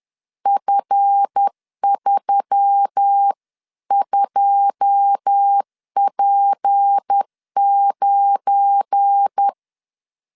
Morse code, heard as a keyed tone, about 10 wpm, 800 Hz. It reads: F32P9
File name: LEZIONI / 10 SIGNS / 10A SLASH 10wpm 2808